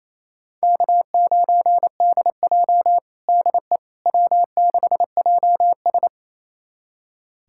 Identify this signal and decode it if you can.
Morse code, heard as a keyed tone, about 28 wpm, 700 Hz. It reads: K9DJ DE W6JS
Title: USA Callsign DE Callsign - 4 Characters - 28wpm